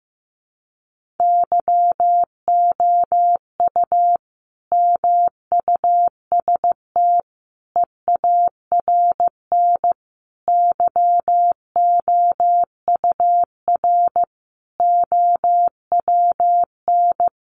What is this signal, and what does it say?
Morse code, heard as a keyed tone, about 15 wpm, 700 Hz. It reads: YOU MUST EARN YOUR OWN
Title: Sentences from Top 1500 Words 15wpm